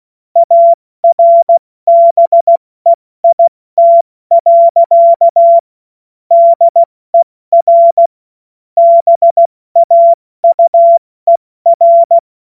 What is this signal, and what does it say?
Morse code, heard as a keyed tone, about 16 wpm, spaced as for 14 wpm, 680 Hz. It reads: ARBEIT. DER BAUER
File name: Witz-16-September-2025-70-F80